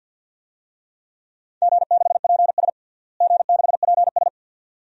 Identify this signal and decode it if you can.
Morse code, heard as a keyed tone, about 52 wpm, 700 Hz. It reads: G6PS G6PS
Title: DXCC-100 Call Signs - 4 Characters - Repeated 2x - 50wpm